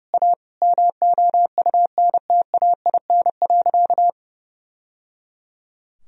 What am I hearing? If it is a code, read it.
Morse code, heard as a keyed tone, about 30 wpm, 700 Hz. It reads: A MOUNTAIN.
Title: Sentences from Top 1000 Words - Review Part 3 - 30wpm